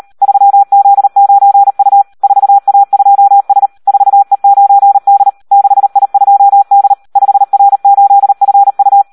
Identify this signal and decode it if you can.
Morse code, heard as a keyed tone, about 38 wpm, 800 Hz. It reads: 3Z9U 4A2S 4E9D 6I2D 5R8FU
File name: Audio practica CW / SD 1 38